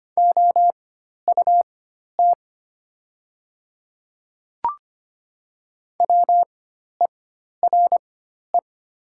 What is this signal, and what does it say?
Morse code, heard as a keyed tone, about 25 wpm, spaced as for 12 wpm, 700 Hz. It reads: OUT WERE
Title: 4x Character Spacing - Top 200 Words - 25wpm